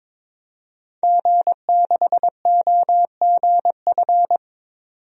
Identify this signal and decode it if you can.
Morse code, heard as a keyed tone, about 22 wpm, 700 Hz. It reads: G6OGF
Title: DXCC-100 Call Signs - 5 Characters - 22wpm